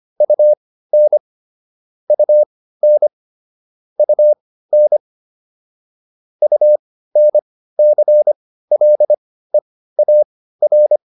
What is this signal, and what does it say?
Morse code, heard as a keyed tone, about 25 wpm, spaced as for 15 wpm, 600 Hz. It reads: UN UN UN UNCLEAR